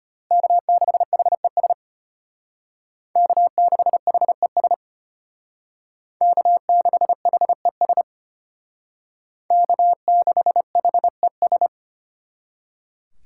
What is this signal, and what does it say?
Morse code, heard as a keyed tone, about 33 wpm, 700 Hz. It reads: K6HES K6HES K6HES K6HES